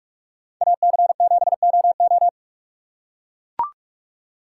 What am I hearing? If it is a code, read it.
Morse code, heard as a keyed tone, about 44 wpm, 700 Hz. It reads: AC7OO